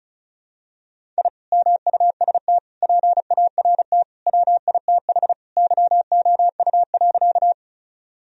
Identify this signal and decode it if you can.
Morse code, heard as a keyed tone, about 36 wpm, 700 Hz. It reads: I MUST PART WITH YOU.
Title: Sentences from Top 1000 Words - Review Part 1 - 35wpm